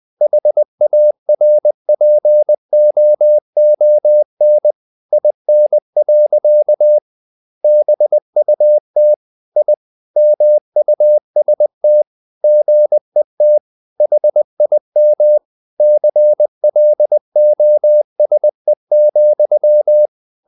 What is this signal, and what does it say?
Morse code, heard as a keyed tone, about 20 wpm, 600 Hz. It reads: HARPOON IN. BUT I MUST GET HIM CLOSE,
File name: the old man and the sea part 2 20wpm